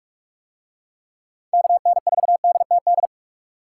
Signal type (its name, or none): Morse code